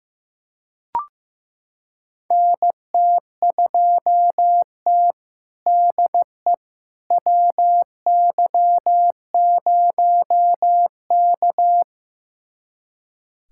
Morse code, heard as a keyed tone, about 15 wpm, 700 Hz. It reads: NT2T DE WY0K
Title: USA Callsign DE Callsign - 4 Characters - 15wpm